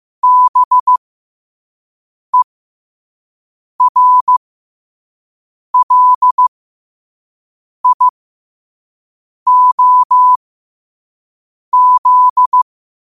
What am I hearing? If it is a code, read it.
Morse code, heard as a keyed tone, about 15 wpm, spaced as for 5 wpm, 1000 Hz. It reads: BERLIOZ
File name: qotdfiles / 5wpm